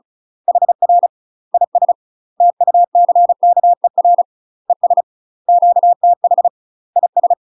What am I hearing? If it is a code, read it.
Morse code, heard as a keyed tone, about 36 wpm, 700 Hz. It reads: HR IS TUCKER ES QTH IS
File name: UK MASTER QSOs TRAINING / 35WPM / UK MASTER QSOs 0035